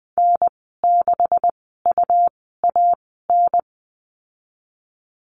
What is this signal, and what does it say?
Morse code, heard as a keyed tone, about 20 wpm, spaced as for 14 wpm, 700 Hz. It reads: N6UAN
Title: 161 Lesson 26g - Letter g - Call Signs 20wpm